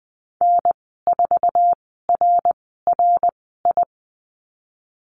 Morse code, heard as a keyed tone, about 20 wpm, spaced as for 15 wpm, 700 Hz. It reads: N4RRI
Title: 076 Lesson 13g - Letter L - Call Signs 20wpm